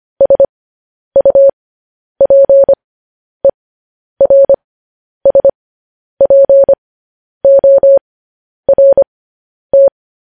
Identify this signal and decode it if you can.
Morse code, heard as a keyed tone, about 25 wpm, spaced as for 10 wpm, 550 Hz. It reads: SUPERSPORT
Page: News Headlines in Morse Code at 10 WPM